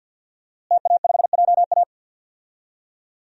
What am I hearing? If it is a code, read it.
Morse code, heard as a keyed tone, about 51 wpm, 700 Hz. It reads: TA5JA
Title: DXCC-100 Call Signs - 5 Characters - Repeated 2x - 50wpm